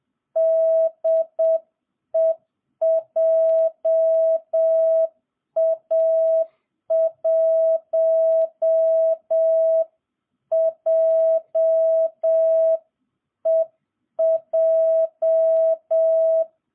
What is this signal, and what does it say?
Morse code, heard as a keyed tone, about 7 wpm, 660 Hz. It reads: DE JA1JEJ